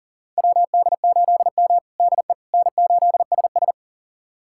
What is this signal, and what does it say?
Morse code, heard as a keyed tone, about 40 wpm, 700 Hz. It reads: WD8M DE N8SS
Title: USA Callsign DE Callsign - 4 Characters - 40wpm